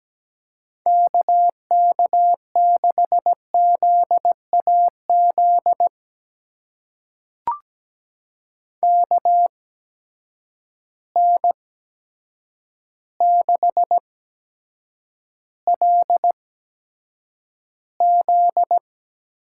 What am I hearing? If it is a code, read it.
Morse code, heard as a keyed tone, about 17 wpm, spaced as for 8 wpm, 700 Hz. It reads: KK6ZAZ KN6LZ
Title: USA Call Signs - 8x Character Spacing - 17wpm